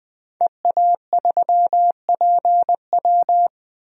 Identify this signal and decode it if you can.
Morse code, heard as a keyed tone, about 20 wpm, 700 Hz. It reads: EA3PW
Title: DXCC-100 Call Signs - 5 Characters - 20wpm